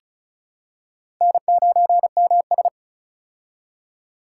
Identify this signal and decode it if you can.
Morse code, heard as a keyed tone, about 36 wpm, 700 Hz. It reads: N9MS
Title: K1USN SST QSO Practice - 35wpm